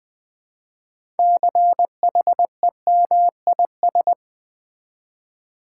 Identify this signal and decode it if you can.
Morse code, heard as a keyed tone, about 20 wpm, 700 Hz. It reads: CHEMIS
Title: Build A Word - 3 Letters At A Time - 9-Letter Words - 20wpm